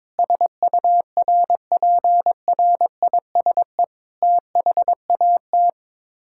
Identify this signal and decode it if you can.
Morse code, heard as a keyed tone, about 22 wpm, 700 Hz. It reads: SURPRISE THAT
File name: practice / 22wpm / Sets of 7 Words - 22wpm